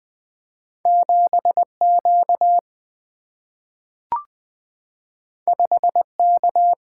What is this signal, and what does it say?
Morse code, heard as a keyed tone, about 20 wpm, 700 Hz. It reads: 7 Q 5 K
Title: Single-Letter Single-Number 20wpm